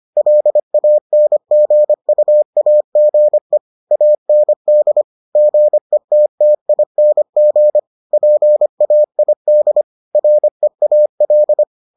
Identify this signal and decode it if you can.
Morse code, heard as a keyed tone, about 25 wpm, 600 Hz. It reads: LANGUAGE AND GETTING PAID REAL